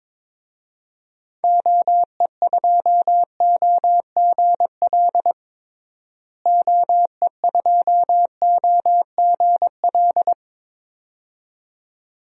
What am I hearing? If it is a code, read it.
Morse code, heard as a keyed tone, about 22 wpm, 700 Hz. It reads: OE2OGL OE2OGL
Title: DXCC-100 Call Signs - 6 Characters - Repeated 2x - 22wpm